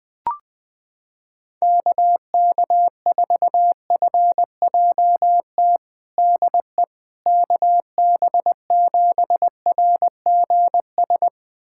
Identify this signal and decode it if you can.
Morse code, heard as a keyed tone, about 20 wpm, 700 Hz. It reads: KK4FJT DE KB7RGS
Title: USA Callsign DE Callsign - 6 Characters - 20wpm